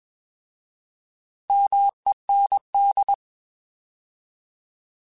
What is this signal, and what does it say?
Morse code, heard as a keyed tone, about 21 wpm, 800 Hz. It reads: MEND
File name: Alphabetical / Alphabetical mp3s / Ms @20